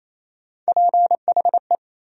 Morse code, heard as a keyed tone, about 28 wpm, 700 Hz. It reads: PHE
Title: USA Call Sign Postfixes - 3 Letters 28wpm